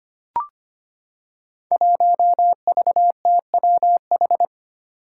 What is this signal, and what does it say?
Morse code, heard as a keyed tone, about 25 wpm, 700 Hz. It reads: 1VTWH
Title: Five Challenging Letter-Number Combinations 25wpm